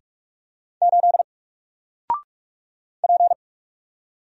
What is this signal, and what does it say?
Morse code, heard as a keyed tone, about 44 wpm, 700 Hz. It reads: 8 P
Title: Single Letter-Number Alt 45wpm